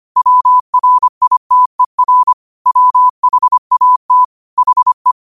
Morse code, heard as a keyed tone, about 25 wpm, 1000 Hz. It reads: WRITER WHAT HE
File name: qotdfiles / 25wpm